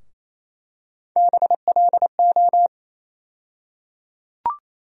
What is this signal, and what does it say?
Morse code, heard as a keyed tone, about 28 wpm, 700 Hz. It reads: BLO
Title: USA Call Sign Postfixes - 3 Letters 28wpm